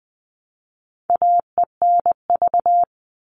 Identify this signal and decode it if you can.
Morse code, heard as a keyed tone, about 20 wpm, 700 Hz. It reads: AENV
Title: All Cut Numbers - 4-Digit 20wpm